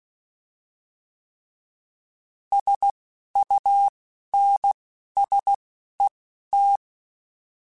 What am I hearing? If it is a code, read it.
Morse code, heard as a keyed tone, about 16 wpm, spaced as for 11 wpm, 800 Hz. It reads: SUNSET